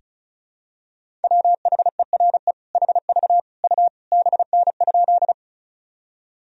Morse code, heard as a keyed tone, about 36 wpm, 700 Hz. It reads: WHERE HV U BN?
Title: Sentences from Top 200 Words and Abbreviations 35wpm